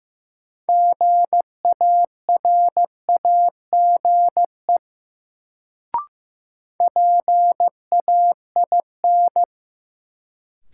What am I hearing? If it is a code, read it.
Morse code, heard as a keyed tone, about 16 wpm, 700 Hz. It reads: GARAGE PAIN
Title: Sentences from Top 1600 Words - New Word Drill - 15wpm